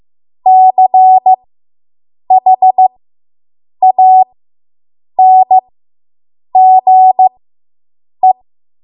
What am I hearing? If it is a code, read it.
Morse code, heard as a keyed tone, about 15 wpm, spaced as for 7 wpm, 750 Hz. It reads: CHANGE